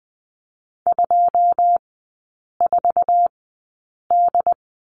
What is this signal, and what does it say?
Morse code, heard as a keyed tone, about 19 wpm, spaced as for 8 wpm, 700 Hz. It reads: 24D
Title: 100 Lesson 17e - Letter u - Three Characters 20wpm